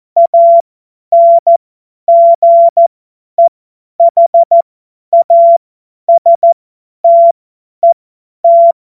Morse code, heard as a keyed tone, about 14 wpm, spaced as for 10 wpm, 680 Hz. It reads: ANGEHASTET
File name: Morsekurs / Witze / Witz 20 Februar 2024 50 F70